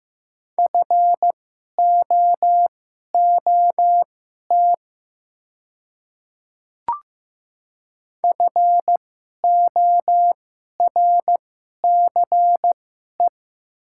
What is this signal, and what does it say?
Morse code, heard as a keyed tone, about 15 wpm, spaced as for 11 wpm, 700 Hz. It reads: FOOT FORCE
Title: 2x Character Spacing - Top 500 Words - 15wpm